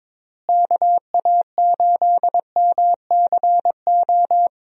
Morse code, heard as a keyed tone, about 22 wpm, 700 Hz. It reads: KA8MCO
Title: USA Call Signs - 6 Characters - Repeated 3x - 22wpm